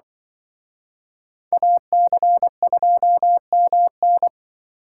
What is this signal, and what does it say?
Morse code, heard as a keyed tone, about 24 wpm, 700 Hz. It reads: AC2MN
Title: Speed Racing - USA 2x2 Call Signs 20wpm